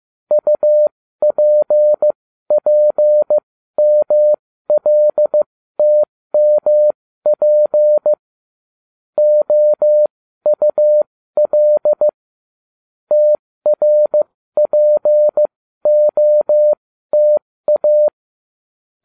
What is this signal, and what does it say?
Morse code, heard as a keyed tone, about 15 wpm, spaced as for 13 wpm, 600 Hz. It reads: UPPMLTMP OUL TRPOTA